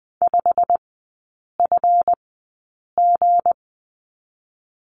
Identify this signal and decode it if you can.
Morse code, heard as a keyed tone, about 20 wpm, spaced as for 8 wpm, 690 Hz. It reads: 5FG
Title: 159 Lesson 26e - Letter g - Three Characters 20wpm